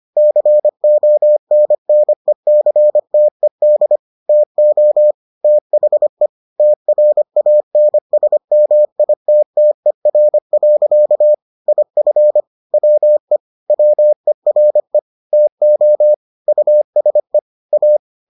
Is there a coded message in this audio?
Yes: Morse code from 0.2 to 18.0 s, heard as a keyed tone, about 25 wpm, 600 Hz. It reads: CONNECTED TO THE TRANSMITTER. IF WE WERE TO USE A